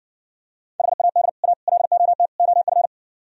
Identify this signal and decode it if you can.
Morse code, heard as a keyed tone, about 59 wpm, 700 Hz. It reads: HAD A LOT OF